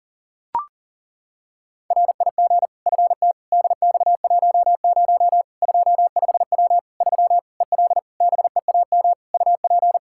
Morse code, heard as a keyed tone, about 40 wpm, 700 Hz. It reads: RIG FT DX10 25W 3 EL BEAM UP
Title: Practice QSO - Exchange 2 - Info - 40wpm